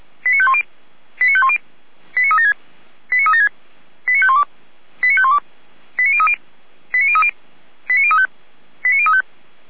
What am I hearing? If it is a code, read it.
Five-tone selective call (ZVEI), heard as quick runs of five tones, tones of about 70 ms. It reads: 89510 89510 89376 89376 89512 89512 89030 89030 89035 89035